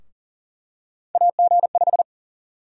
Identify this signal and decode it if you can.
Morse code, heard as a keyed tone, about 40 wpm, 700 Hz. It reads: AG5
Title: USA Call Sign Prefixes - 40wpm